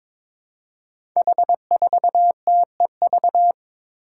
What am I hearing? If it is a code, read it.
Morse code, heard as a keyed tone, about 22 wpm, 700 Hz. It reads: H4TEV
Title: Five Challenging Letter-Number Combinations 22wpm